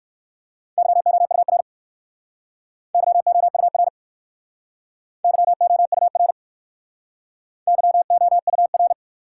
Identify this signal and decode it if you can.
Morse code, heard as a keyed tone, about 60 wpm, 700 Hz. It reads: YOUR YOUR YOUR YOUR